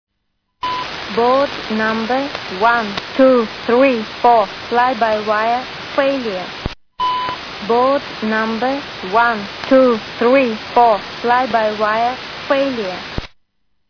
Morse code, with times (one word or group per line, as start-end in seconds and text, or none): none